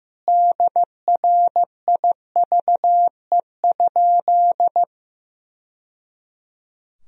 Morse code, heard as a keyed tone, about 15 wpm, 700 Hz. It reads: DRIVE?
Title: Sentences from Top 1000 Words - Review Part 3 - 15wpm